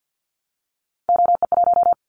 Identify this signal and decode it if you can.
Morse code, heard as a keyed tone, about 50 wpm, 700 Hz. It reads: OE1
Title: Callsign Prefixes for Top 25 DXCC - 50wpm